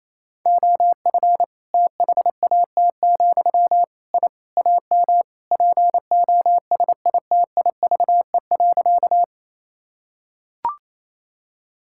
Morse code, heard as a keyed tone, about 28 wpm, 700 Hz. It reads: OF THAT, I AM POSITIVE.
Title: Sentences from Top 1000 Words - Review Part 5 - 28wpm